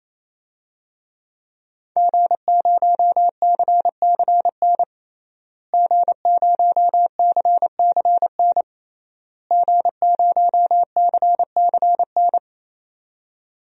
Morse code, heard as a keyed tone, about 27 wpm, 700 Hz. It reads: G0CCN G0CCN G0CCN
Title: DXCC-100 Call Signs - 5 Characters - Repeated 3x - 28wpm